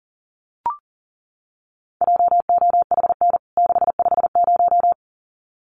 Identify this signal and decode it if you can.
Morse code, heard as a keyed tone, about 40 wpm, 700 Hz. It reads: JOHN 650
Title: CWT Exchange 40wpm